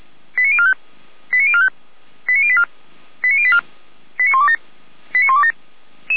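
Five-tone selective call (ZVEI), heard as quick runs of five tones, tones of about 70 ms. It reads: 89045 89045 89084 89084 89127 89127
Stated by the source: ZVEI five-tone selective call 89081 received by SDR, from fire brigade Feuerwehr Orlamünde's audio server.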